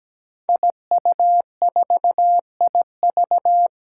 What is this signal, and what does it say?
Morse code, heard as a keyed tone, about 17 wpm, 700 Hz. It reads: IU4IV
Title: Five Challenging Letter-Number Combinations 17wpm